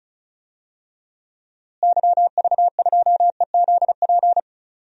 Morse code, heard as a keyed tone, about 34 wpm, 700 Hz. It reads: YV2EZP
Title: DXCC-100 Call Signs - 6 Characters - Repeated 3x - 35wpm